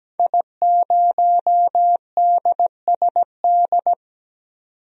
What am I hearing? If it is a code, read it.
Morse code, heard as a keyed tone, about 17 wpm, 700 Hz. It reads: I0DSD
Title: DXCC-100 Call Signs - 5 Characters - 17wpm